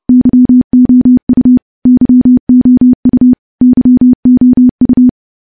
Morse code, heard as a keyed tone, about 30 wpm, 250 Hz. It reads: YOU YOU YOU
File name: morse / morse-training-30WPM